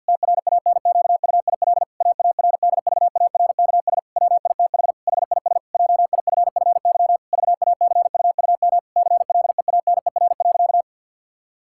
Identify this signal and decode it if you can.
Morse code, heard as a keyed tone, about 50 wpm, 700 Hz. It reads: TRANQUIL AARDVARKS WITH HIS JIFFY VACUUM CLEANER.
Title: Pangram Challenge v2 - 50wpm